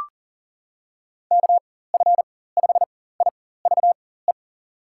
Morse code, heard as a keyed tone, about 37 wpm, spaced as for 18 wpm, 700 Hz. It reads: KF5IVE
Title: USA Call Signs - 6 Characters - 4x Character Spacing - 40wpm